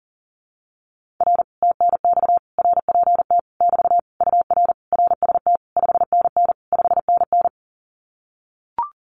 Morse code, heard as a keyed tone, about 40 wpm, 700 Hz. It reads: R TNX RPT <BT> UR RST 5NN 5NN
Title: Common QSO Phrases Encore 40wpm